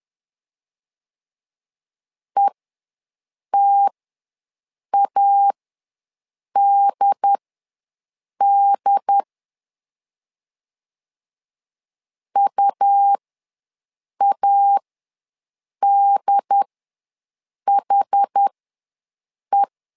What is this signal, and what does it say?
Morse code, heard as a keyed tone, about 10 wpm, spaced as for 6 wpm, 800 Hz. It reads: ETADD UADHE
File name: LEZIONI / 2 UHDN / 2 UHD ws 10wpm 2058